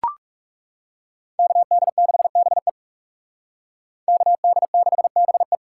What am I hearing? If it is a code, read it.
Morse code, heard as a keyed tone, about 34 wpm, 700 Hz. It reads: KD6BE KD6BE
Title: Speed Racing - USA 2x2 Call Signs 30wpm